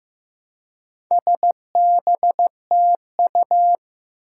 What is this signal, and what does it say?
Morse code, heard as a keyed tone, about 15 wpm, 700 Hz. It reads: SBTU
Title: Four Challenging Letter-Number Combinations 15wpm